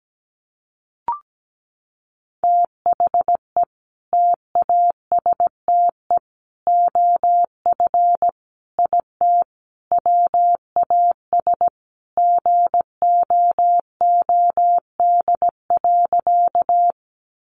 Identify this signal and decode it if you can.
Morse code, heard as a keyed tone, about 17 wpm, 700 Hz. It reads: THE TASTE OF IT WAS GOOD.
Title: Sentences from Top 1100 Words 17wpm